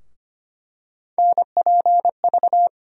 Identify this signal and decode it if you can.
Morse code, heard as a keyed tone, about 25 wpm, 700 Hz. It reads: NPV
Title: USA Call Sign Postfixes - 3 Letters 25wpm